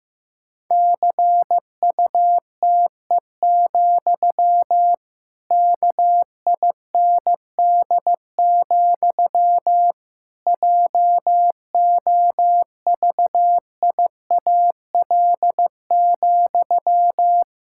Morse code, heard as a keyed tone, about 15 wpm, 700 Hz. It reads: CUTE, KIND, JOVIAL,
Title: Pangram Challenge v2 - 15wpm